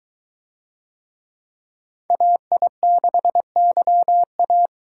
Morse code, heard as a keyed tone, about 21 wpm, 700 Hz. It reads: AI6YA